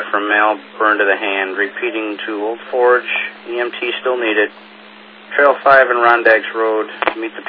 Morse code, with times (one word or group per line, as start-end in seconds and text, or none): none